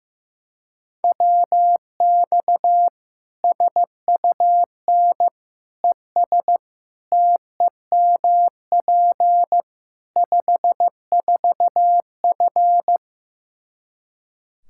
Morse code, heard as a keyed tone, about 15 wpm, 700 Hz. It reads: WX SUN ES TEMP 54F
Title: Practice QSO - Exchange 2 - Info - 15wpm